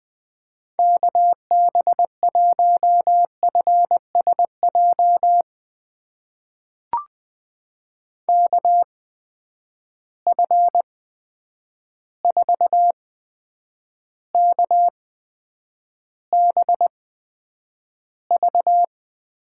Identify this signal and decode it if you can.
Morse code, heard as a keyed tone, about 20 wpm, spaced as for 9 wpm, 700 Hz. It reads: KB1FSJ KF4KBV
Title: USA Call Signs - 6 Characters - 8x Character Spacing - 20wpm